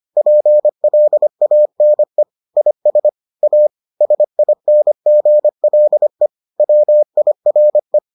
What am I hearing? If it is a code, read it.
Morse code, heard as a keyed tone, about 25 wpm, 600 Hz. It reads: PLANE IS A SINGLE WIRE